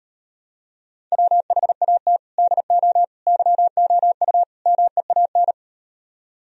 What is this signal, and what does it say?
Morse code, heard as a keyed tone, about 38 wpm, 700 Hz. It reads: WHAT DO YOU MEAN